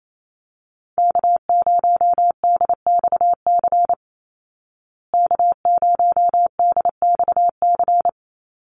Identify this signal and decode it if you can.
Morse code, heard as a keyed tone, about 28 wpm, 700 Hz. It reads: K0DXC K0DXC
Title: Call Signs Repeated 3x - 28wpm